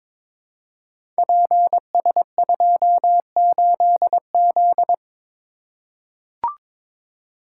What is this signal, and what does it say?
Morse code, heard as a keyed tone, about 22 wpm, 700 Hz. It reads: PS28Z
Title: Code Groups - 5 Letter:Numbers - 22wpm